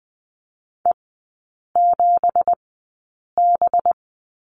Morse code, heard as a keyed tone, about 20 wpm, spaced as for 8 wpm, 700 Hz. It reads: E7B
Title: 184 Lesson 30e - Letter b - Three Characters 20wpm